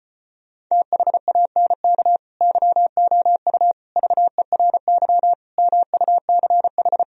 Morse code, heard as a keyed tone, about 33 wpm, 700 Hz. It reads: THANK YOU VERY MUCH